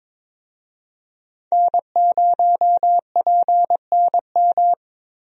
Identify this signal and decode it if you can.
Morse code, heard as a keyed tone, about 21 wpm, 700 Hz. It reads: N0PNM
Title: USA Call Signs - 5 Characters - Repeated 2x - 22wpm